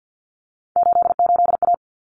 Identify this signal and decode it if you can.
Morse code, heard as a keyed tone, about 48 wpm, 700 Hz. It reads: 88A
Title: Common Cut Numbers - 3-Digit 50wpm